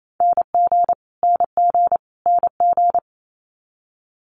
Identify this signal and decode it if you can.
Morse code, heard as a keyed tone, about 28 wpm, 700 Hz. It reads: NG NG NG
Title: Multi-letter Phonemes Repeated 3x 28wpm